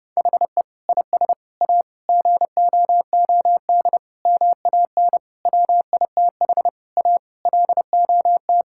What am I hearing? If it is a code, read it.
Morse code, heard as a keyed tone, about 30 wpm, 700 Hz. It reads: HE IS A GOOD MAN WITH A LOT
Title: Sentences from Top 1000 Words - Review Part 6 - 30wpm